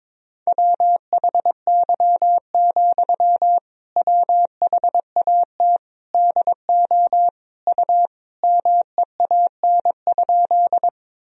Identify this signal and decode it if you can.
Morse code, heard as a keyed tone, about 22 wpm, 700 Hz. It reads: WHY, WHAT DO U MEAN?
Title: Sentences from Top 200 Words and Abbreviations 22wpm